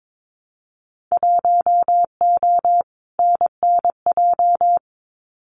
Morse code, heard as a keyed tone, about 22 wpm, 700 Hz. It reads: 1O NNJ
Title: Winter Field Day Exchanges 22wpm